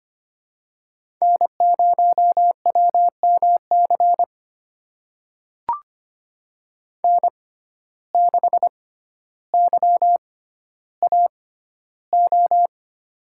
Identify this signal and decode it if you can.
Morse code, heard as a keyed tone, about 25 wpm, spaced as for 13 wpm, 690 Hz. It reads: N0WMC N6YAO
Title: USA Call Signs - 5 Characters - 6x Character Spacing - 25wpm